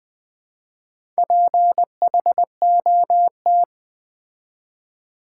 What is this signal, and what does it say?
Morse code, heard as a keyed tone, about 20 wpm, 700 Hz. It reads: PHOT
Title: Build A Word - 4 Letters At A Time - 12-Letter Words - 20wpm